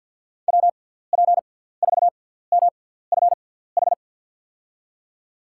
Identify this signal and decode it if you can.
Morse code, heard as a keyed tone, about 53 wpm, 700 Hz. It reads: W P 4 M F H
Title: USA Call Signs - 6 Characters - 6x Character Spacing - 50wpm